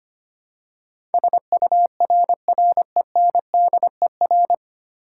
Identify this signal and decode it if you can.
Morse code, heard as a keyed tone, about 25 wpm, 700 Hz. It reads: SURRENDER
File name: Nine Letter Words - 25wpm